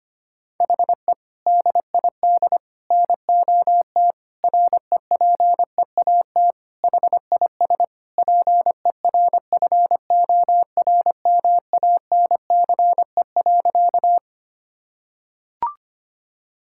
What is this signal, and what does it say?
Morse code, heard as a keyed tone, about 25 wpm, 700 Hz. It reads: HE DID NOT REPEAT HIS PERFORMANCE.
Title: Sentences from Top 1600 Words 25wpm